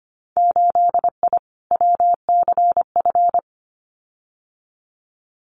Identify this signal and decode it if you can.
Morse code, heard as a keyed tone, about 25 wpm, 700 Hz. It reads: 8I WCF
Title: Winter Field Day Exchanges Encore 25wpm